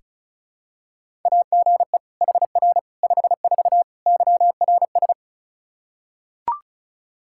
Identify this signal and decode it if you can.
Morse code, heard as a keyed tone, about 36 wpm, 700 Hz. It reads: AGE HR 54 YRS
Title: Practice QSO - Exchange 3 - More Info - 35wpm